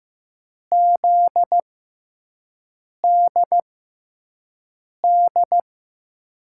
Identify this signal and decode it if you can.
Morse code, heard as a keyed tone, about 15 wpm, spaced as for 5 wpm, 700 Hz. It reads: ZDD
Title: USA Call Signs - 6 Characters - 6x Character Spacing - 15wpm